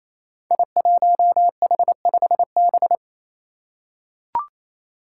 Morse code, heard as a keyed tone, about 27 wpm, 700 Hz. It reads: I1H5B
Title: Five Challenging Letter-Number Combinations 28wpm